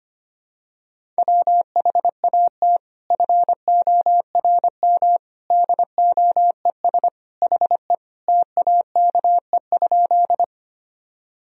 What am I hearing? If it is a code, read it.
Morse code, heard as a keyed tone, about 25 wpm, 700 Hz. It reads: WHAT FORM DOES HE TAKE?
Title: Sentences from Top 1000 Words - Review Part 1 - 25wpm